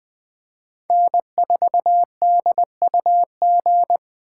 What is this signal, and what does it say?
Morse code, heard as a keyed tone, about 20 wpm, 700 Hz. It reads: N4DUG
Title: USA Call Signs - 2x Character Spacing - 20wpm